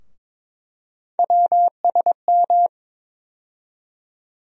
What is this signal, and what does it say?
Morse code, heard as a keyed tone, about 22 wpm, 700 Hz. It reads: WSM
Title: USA Call Sign Postfixes - 3 Letters 22wpm